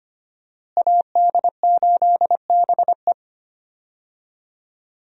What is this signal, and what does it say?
Morse code, heard as a keyed tone, about 25 wpm, 700 Hz. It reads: AD8BE
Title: USA Call Signs - 5 Characters - Repeated 2x - 25wpm